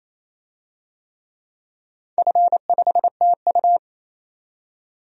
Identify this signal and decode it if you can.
Morse code, heard as a keyed tone, about 27 wpm, 690 Hz. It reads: F5TU
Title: DXCC-100 Call Signs - 4 Characters - 28wpm